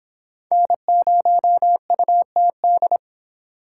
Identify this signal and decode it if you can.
Morse code, heard as a keyed tone, about 32 wpm, 700 Hz. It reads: N0UTD